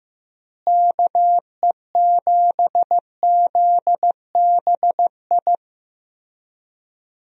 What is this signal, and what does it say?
Morse code, heard as a keyed tone, about 15 wpm, 700 Hz. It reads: KE7ZBI
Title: USA Call Signs - 6 Characters - Repeated 2x - 15wpm